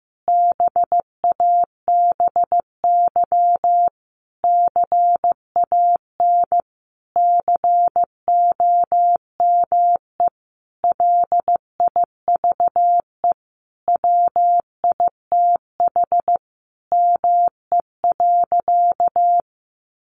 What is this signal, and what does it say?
Morse code, heard as a keyed tone, about 15 wpm, 700 Hz. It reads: BABY CAN COME LIVE WITH ME.
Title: Sentences from Top 1300 Words 15wpm